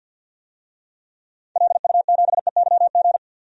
Morse code, heard as a keyed tone, about 50 wpm, 700 Hz. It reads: LU7EYG
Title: DXCC-100 Call Signs - 6 Characters - 50wpm